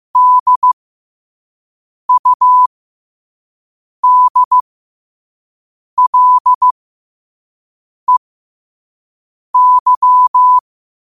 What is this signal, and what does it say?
Morse code, heard as a keyed tone, about 15 wpm, spaced as for 5 wpm, 1000 Hz. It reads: DUDLEY
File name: qotdfiles / 5wpm